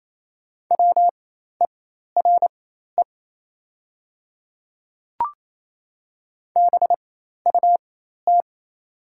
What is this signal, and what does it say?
Morse code, heard as a keyed tone, about 27 wpm, spaced as for 13 wpm, 690 Hz. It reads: WERE BUT
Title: 4x Character Spacing - Top 100 Words - 28wpm